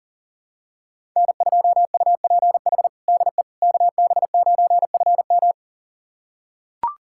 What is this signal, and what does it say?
Morse code, heard as a keyed tone, about 40 wpm, 700 Hz. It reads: N2UPH DE KB9FM
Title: USA Callsign DE Callsign - 5 Characters - 40wpm